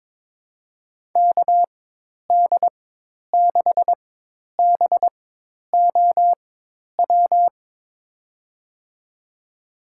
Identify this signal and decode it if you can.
Morse code, heard as a keyed tone, about 21 wpm, spaced as for 10 wpm, 700 Hz. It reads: KD6BOW
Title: USA Call Signs - 4x Character Spacing - 22wpm